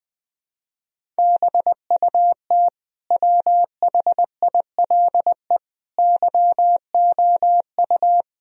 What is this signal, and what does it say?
Morse code, heard as a keyed tone, about 21 wpm, 700 Hz. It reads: BUT WHILE YOU
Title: Sentences from Top 1000 Words - Review Part 1 - 20wpm